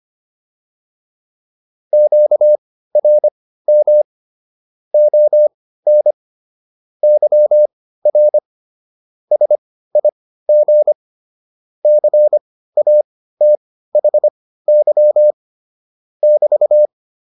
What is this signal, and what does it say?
Morse code, heard as a keyed tone, about 25 wpm, spaced as for 15 wpm, 600 Hz. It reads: QRM ON YR SIG CATHY <BT>